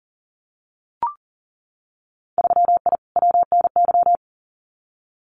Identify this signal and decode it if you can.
Morse code, heard as a keyed tone, about 40 wpm, 700 Hz. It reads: 3I WNY
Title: Winter Field Day Exchanges Encore 40wpm